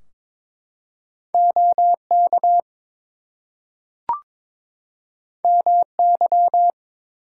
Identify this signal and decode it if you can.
Morse code, heard as a keyed tone, about 22 wpm, 700 Hz. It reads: OK MY
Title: Two Letter Words - 22wpm